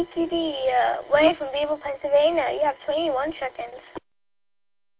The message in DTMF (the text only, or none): none